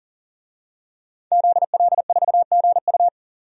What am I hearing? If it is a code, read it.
Morse code, heard as a keyed tone, about 40 wpm, 700 Hz. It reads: ZL4GU
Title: DXCC-100 Call Signs - 5 Characters - 40wpm